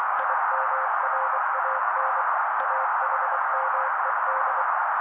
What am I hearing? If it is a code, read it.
Morse code, heard as a keyed tone, about 24 wpm, 530 Hz. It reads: IMRAN AHMED
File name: daily / 23wpm noise